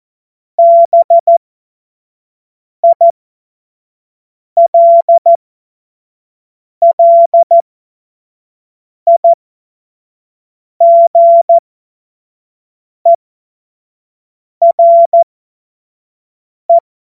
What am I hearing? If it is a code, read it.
Morse code, heard as a keyed tone, about 14 wpm, spaced as for 5 wpm, 680 Hz. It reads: BILLIGERE
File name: Morsekurs / Witze / Witz 25. April 2023 25 F70